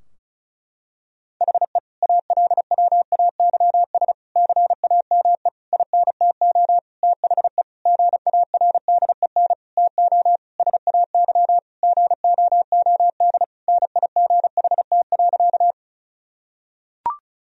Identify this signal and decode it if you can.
Morse code, heard as a keyed tone, about 36 wpm, 700 Hz. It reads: HE ALWAYS CAME INTO THE GARDEN TO SAY GOOD NIGHT.
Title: Sentences from Top 1000 Words - Review Part 4 - 35wpm